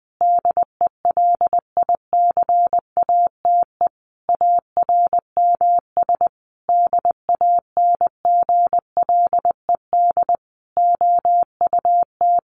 Morse code, heard as a keyed tone, about 20 wpm, 700 Hz. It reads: DELICATE ARMS DANGLED OUT
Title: A Sound Of Thunder 20wpm